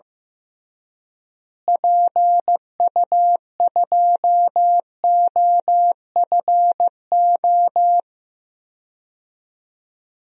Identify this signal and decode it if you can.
Morse code, heard as a keyed tone, about 15 wpm, 700 Hz. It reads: PU2OFO
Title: DXCC-100 Call Signs - 6 Characters - Repeated 3x - 15wpm